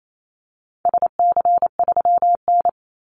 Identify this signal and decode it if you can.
Morse code, heard as a keyed tone, about 28 wpm, 700 Hz. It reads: SC3N
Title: Call Signs - 28wpm